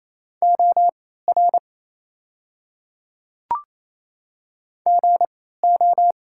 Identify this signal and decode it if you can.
Morse code, heard as a keyed tone, about 28 wpm, spaced as for 16 wpm, 700 Hz. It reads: OR GO
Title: 3x Character Spacing - Top 100 Words - 28wpm